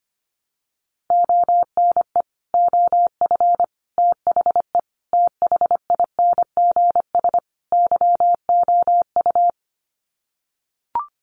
Morse code, heard as a keyed tone, about 25 wpm, 700 Hz. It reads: ONE OF THE THINGS YOU
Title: Sets of 5 Words Encore2 25wpm